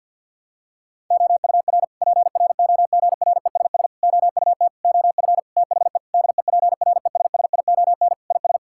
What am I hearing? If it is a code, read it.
Morse code, heard as a keyed tone, about 50 wpm, 700 Hz. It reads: OUR PROGRESS OUT OF THE DEPRESSION IS
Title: Sentences from Top 1600 Words 50wpm